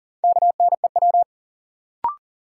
Morse code, heard as a keyed tone, about 40 wpm, 700 Hz. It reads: KNEW
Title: Sentences from Top 400 Words - New Word Drill - 40wpm